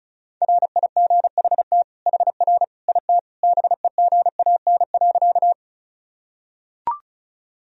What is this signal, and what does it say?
Morse code, heard as a keyed tone, about 34 wpm, 700 Hz. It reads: RIGHT HR IT BEGAN.
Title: Sentences from Top 300 Words and Abbreviations 35wpm